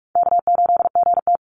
Morse code, heard as a keyed tone, about 44 wpm, 700 Hz. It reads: K8GT
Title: Call Signs Repeated 2x - 45wpm